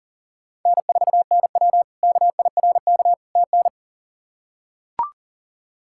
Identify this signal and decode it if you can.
Morse code, heard as a keyed tone, about 37 wpm, 700 Hz. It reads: N4NW KIRK TN
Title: K1USN SST QSO Practice - 40wpm